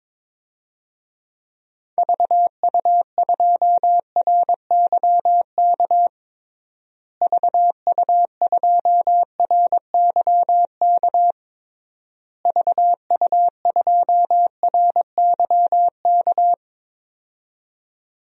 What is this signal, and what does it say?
Morse code, heard as a keyed tone, about 22 wpm, 700 Hz. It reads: VU2RYK VU2RYK VU2RYK
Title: DXCC-100 Call Signs - 6 Characters - Repeated 3x - 22wpm